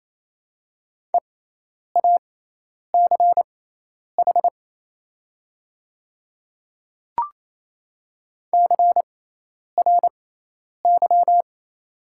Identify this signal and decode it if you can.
Morse code, heard as a keyed tone, about 28 wpm, spaced as for 10 wpm, 700 Hz. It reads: EACH CRY
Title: 6x Character Spacing - Top 500 Words - 28wpm